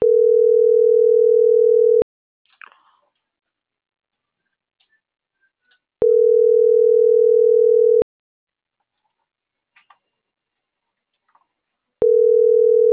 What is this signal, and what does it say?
Ringback tone (North America), 440 + 480 Hz.